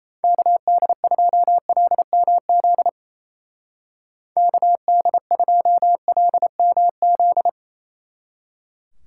Morse code, heard as a keyed tone, about 37 wpm, 700 Hz. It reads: KD2LMZ KD2LMZ